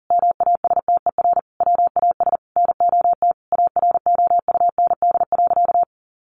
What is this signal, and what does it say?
Morse code, heard as a keyed tone, about 40 wpm, 700 Hz. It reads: MASTER WAS NOT AROUND.
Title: Sentences from Top 1300 Words 40wpm